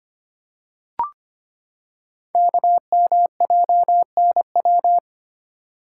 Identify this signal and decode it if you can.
Morse code, heard as a keyed tone, about 25 wpm, 700 Hz. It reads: KMJNW
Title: Code Groups - 5 Letters - 25wpm